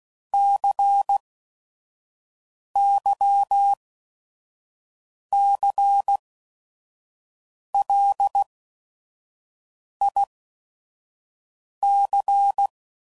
Morse code, heard as a keyed tone, about 16 wpm, spaced as for 5 wpm, 800 Hz. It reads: CYCLIC
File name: CW Code Groups @5WPM / Technical Subjects / XC